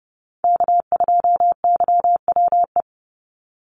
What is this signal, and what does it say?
Morse code, heard as a keyed tone, about 30 wpm, 700 Hz. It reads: K2YWE
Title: Call Signs Repeated 2x - 30wpm